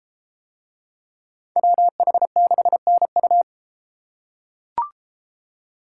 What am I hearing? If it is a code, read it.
Morse code, heard as a keyed tone, about 32 wpm, 700 Hz. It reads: WH6NU